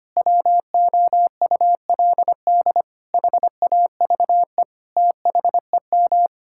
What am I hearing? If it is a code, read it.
Morse code, heard as a keyed tone, about 25 wpm, 700 Hz. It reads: WOULD HAVE THEM
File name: Sets of 9 Words - 25wpm